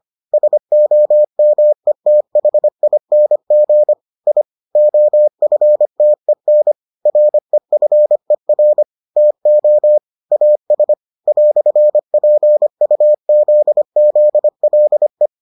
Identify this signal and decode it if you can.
Morse code, heard as a keyed tone, about 25 wpm, 600 Hz. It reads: SOMETHING I OFTEN REFER TO AS "PUZZLE